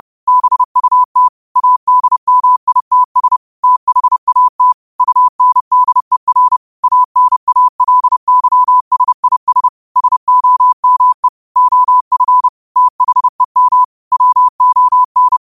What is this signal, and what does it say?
Morse code, heard as a keyed tone, about 30 wpm, 1000 Hz. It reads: BUT ADMITS THAT UNDER ANALYSIS SOME OF THEM WON
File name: qotdfiles / 30wpm